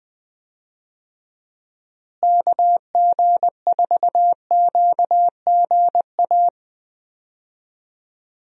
Morse code, heard as a keyed tone, about 20 wpm, 700 Hz. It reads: KG4QGA
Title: USA Call Signs - 6 Characters - Repeated 3x - 20wpm